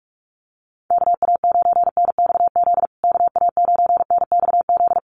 Morse code, heard as a keyed tone, about 44 wpm, 700 Hz. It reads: KA9NXZ KA9NXZ
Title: USA Call Signs Repeated 2x 45wpm